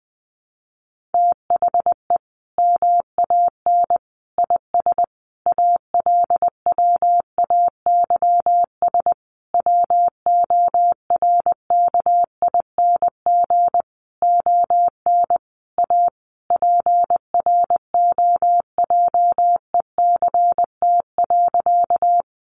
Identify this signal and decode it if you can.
Morse code, heard as a keyed tone, about 20 wpm, 700 Hz. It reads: THE MAN IS ALWAYS WORKING ON A PROJECT.
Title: Sentences from Top 1400 Words 20wpm